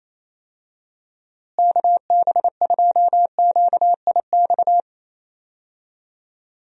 Morse code, heard as a keyed tone, about 28 wpm, 700 Hz. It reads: KB2QIX
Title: USA Call Signs - 6 Characters - Repeated 3x - 28wpm